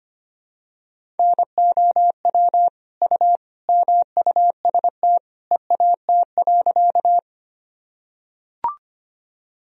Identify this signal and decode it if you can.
Morse code, heard as a keyed tone, about 25 wpm, 700 Hz. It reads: NOW U MUST EAT.
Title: Sentences from Top 300 Words and Abbreviations 25wpm